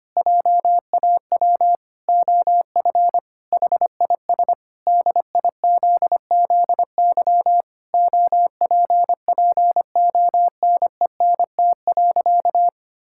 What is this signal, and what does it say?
Morse code, heard as a keyed tone, about 25 wpm, 700 Hz. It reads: JAW OF HIS DIZZY OPPONENT.
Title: Pangram Challenge v2 - 25wpm